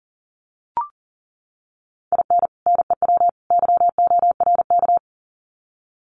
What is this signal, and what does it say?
Morse code, heard as a keyed tone, about 40 wpm, 700 Hz. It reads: IN NEW YORK
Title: Sets of 3 Words 40wpm